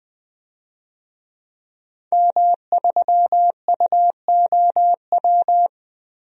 Morse code, heard as a keyed tone, about 20 wpm, 700 Hz. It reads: M3UOW
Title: DXCC-100 Call Signs - 5 Characters - 20wpm